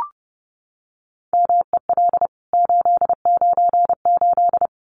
Morse code, heard as a keyed tone, about 30 wpm, 700 Hz. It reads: MEL 898
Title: CWT Exchange 30wpm